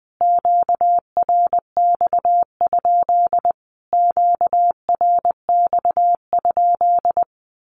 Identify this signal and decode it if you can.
Morse code, heard as a keyed tone, about 20 wpm, 700 Hz. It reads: QRX? QRX?
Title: Q-Codes Repeated 3x 20wpm